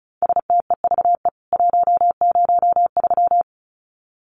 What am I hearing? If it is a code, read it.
Morse code, heard as a keyed tone, about 34 wpm, 700 Hz. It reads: STEVE 103
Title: CWT Exchange 35wpm